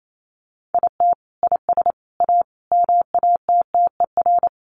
Morse code, heard as a keyed tone, about 28 wpm, 700 Hz. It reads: IT IS A MATTER
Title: Sentences from Top 1100 Words 28wpm